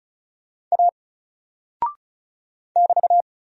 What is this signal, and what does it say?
Morse code, heard as a keyed tone, about 36 wpm, 700 Hz. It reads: A <BT>
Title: Single Character (Letter:Number:Punctuation:Prosign) Alt 35wpm